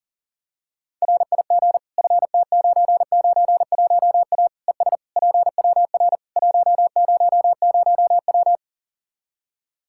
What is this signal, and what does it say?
Morse code, heard as a keyed tone, about 40 wpm, 700 Hz. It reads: RIG FT991A ES PWR 100W
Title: Practice QSO - Exchange 2 - Info - 40wpm